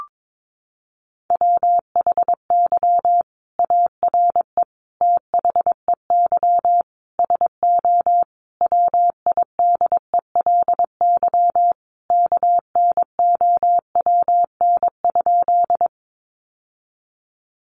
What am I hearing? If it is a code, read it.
Morse code, heard as a keyed tone, about 22 wpm, 700 Hz. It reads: WHY ARE THEY SO WIDELY KNOWN?
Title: Sentences from Top 1200 Words 22wpm